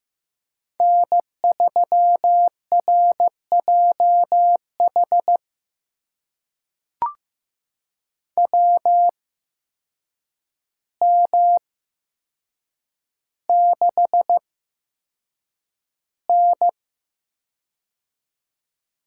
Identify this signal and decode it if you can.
Morse code, heard as a keyed tone, about 15 wpm, spaced as for 7 wpm, 700 Hz. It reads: N3RJH WM6N
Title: USA Call Signs - 8x Character Spacing - 15wpm